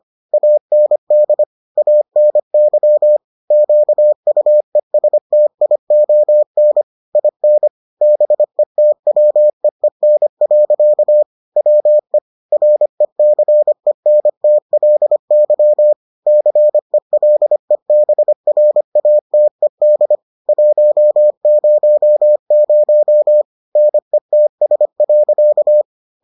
Morse code, heard as a keyed tone, about 25 wpm, 600 Hz. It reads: AND ANY QUESTION IN BETWEEN. WE RECENTLY CELEBRATED 100 NETS.